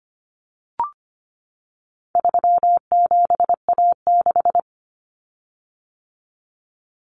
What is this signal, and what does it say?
Morse code, heard as a keyed tone, about 25 wpm, 700 Hz. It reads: 37A6
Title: Common Cut Numbers - 4-Digit 25wpm